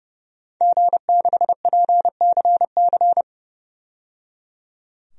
Morse code, heard as a keyed tone, about 30 wpm, 700 Hz. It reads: G6PCC
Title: DXCC-100 Call Signs - 5 Characters - 30wpm